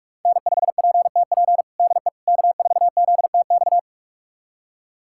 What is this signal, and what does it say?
Morse code, heard as a keyed tone, about 46 wpm, 700 Hz. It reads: N5PTP DE K4ZTX
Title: USA Callsign DE Callsign - 5 Characters - 45wpm